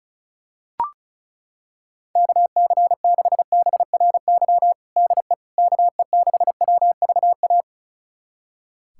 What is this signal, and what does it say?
Morse code, heard as a keyed tone, about 36 wpm, 700 Hz. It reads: KC6BRY DE KE6WVA
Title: USA Callsign DE Callsign - 6 Characters - 35wpm